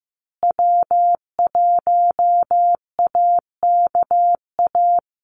Morse code, heard as a keyed tone, about 15 wpm, 700 Hz. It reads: W1AKA
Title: USA Call Signs Repeated 3x 15wpm